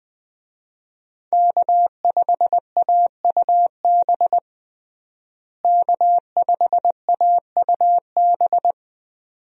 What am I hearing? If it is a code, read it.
Morse code, heard as a keyed tone, about 21 wpm, 700 Hz. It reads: K5AUB K5AUB
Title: USA Call Signs - 5 Characters - Repeated 2x - 20wpm